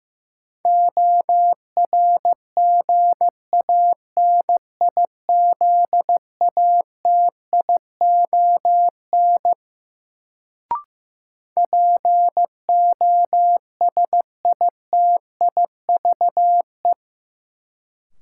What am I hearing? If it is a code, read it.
Morse code, heard as a keyed tone, about 15 wpm, 700 Hz. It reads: ORGANIZATION POSITIVE
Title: Sentences from Top 1000 Words - Review Part 5 - New Word Drill - 15wpm